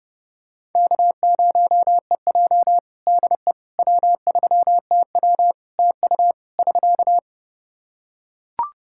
Morse code, heard as a keyed tone, about 30 wpm, 700 Hz. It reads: K0EJ DE W3TW TU <SK>
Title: Practice QSO - Exchange 4 - Ending - 30wpm